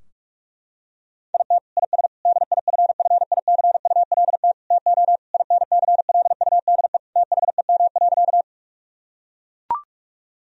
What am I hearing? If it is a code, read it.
Morse code, heard as a keyed tone, about 42 wpm, 700 Hz. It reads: IT IS DIFFICULT TO INCLUDE THEM.
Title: Sentences from Top 1000 Words - Review Part 3 - 45wpm